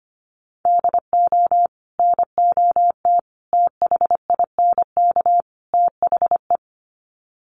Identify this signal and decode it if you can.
Morse code, heard as a keyed tone, about 25 wpm, 700 Hz. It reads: DO NOT THINK THE
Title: Sets of 4 Words 25wpm